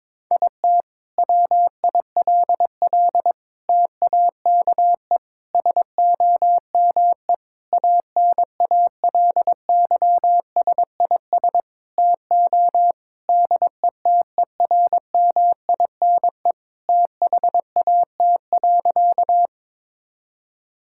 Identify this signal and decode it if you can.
Morse code, heard as a keyed tone, about 22 wpm, 700 Hz. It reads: IT WILL TAKE SOME ANALYSIS TO DETERMINE THAT.
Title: Sentences from Top 1000 Words - Review Part 5 - 22wpm